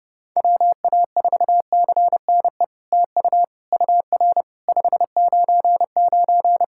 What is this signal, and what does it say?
Morse code, heard as a keyed tone, about 30 wpm, 690 Hz. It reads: WA4CNE TU UR 599
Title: SOTA QSO Practice 30wpm